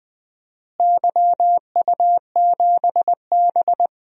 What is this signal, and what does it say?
Morse code, heard as a keyed tone, about 20 wpm, 700 Hz. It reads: YU7B
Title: DXCC-100 Call Signs - 4 Characters - Repeated 2x - 20wpm